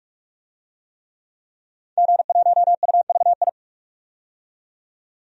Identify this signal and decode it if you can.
Morse code, heard as a keyed tone, about 45 wpm, 700 Hz. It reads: G1UVI
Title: DXCC-100 Call Signs - 5 Characters - 45wpm